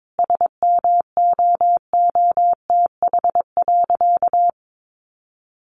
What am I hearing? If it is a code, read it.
Morse code, heard as a keyed tone, about 22 wpm, 700 Hz. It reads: SMOOTH.
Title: Sentences from Top 1400 Words 22wpm